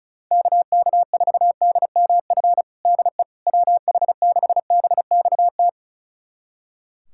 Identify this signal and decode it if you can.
Morse code, heard as a keyed tone, about 36 wpm, 700 Hz. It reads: KK4DMF DE WH6BXT
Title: USA Callsign DE Callsign 35wpm